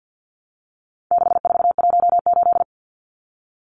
Morse code, heard as a keyed tone, about 50 wpm, 700 Hz. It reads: 6418
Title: Numbers - 4-Digit 50wpm